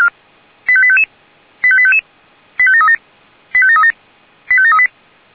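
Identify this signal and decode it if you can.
Five-tone selective call (ZVEI), heard as quick runs of five tones, tones of about 70 ms. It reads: 85855 85855 85737 85737 85738
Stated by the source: ZVEI five-tone call 85853 received by SDR.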